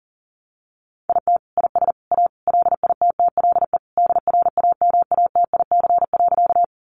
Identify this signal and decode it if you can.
Morse code, heard as a keyed tone, about 37 wpm, 700 Hz. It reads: IT IS A LITTLE DRAMATIC.
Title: Sentences from Top 1300 Words 40wpm